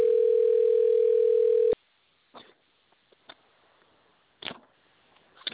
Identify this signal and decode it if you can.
Ringback tone (North America), 440 + 480 Hz.